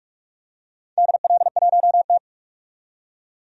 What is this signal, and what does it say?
Morse code, heard as a keyed tone, about 48 wpm, 690 Hz. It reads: DL1T
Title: DXCC-100 Call Signs - 4 Characters - 45wpm